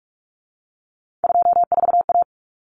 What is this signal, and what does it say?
Morse code, heard as a keyed tone, about 44 wpm, 700 Hz. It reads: 24A
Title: Common Cut Numbers - 3-Digit 45wpm